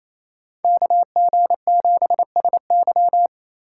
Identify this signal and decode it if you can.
Morse code, heard as a keyed tone, about 33 wpm, 700 Hz. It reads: KG7SY
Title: Speed Racing - USA 2x2 Call Signs 28wpm